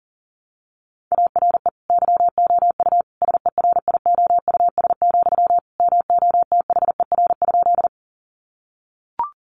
Morse code, heard as a keyed tone, about 40 wpm, 700 Hz. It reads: ARE YOU SERIOUS, MOTHER?
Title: Sentences from Top 1100 Words 40wpm